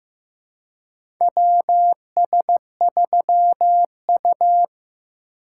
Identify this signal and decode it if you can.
Morse code, heard as a keyed tone, about 15 wpm, 700 Hz. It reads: WS3U
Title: USA Call Signs - 4 Characters - 2x Character Spacing - 15wpm